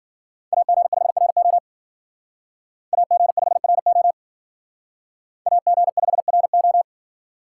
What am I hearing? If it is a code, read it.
Morse code, heard as a keyed tone, about 52 wpm, 700 Hz. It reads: AG5RO AG5RO AG5RO